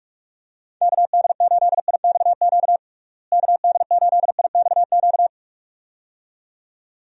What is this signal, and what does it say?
Morse code, heard as a keyed tone, about 47 wpm, 700 Hz. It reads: KD8IXQ KD8IXQ
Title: USA Call Signs - 6 Characters - Repeated 2x - 45wpm